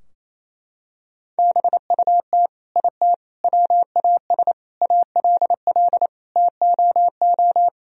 Morse code, heard as a keyed tone, about 28 wpm, 700 Hz. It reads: BUT IT WAS ALL TOO
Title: Sentences from Top 1000 Words - Review Part 4 - 28wpm